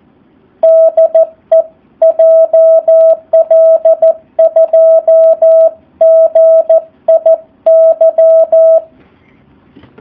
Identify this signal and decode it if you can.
Morse code, heard as a keyed tone, about 14 wpm, 650 Hz. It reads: DE JL2GIY
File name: jh8cbh / 2019cw / SP04- w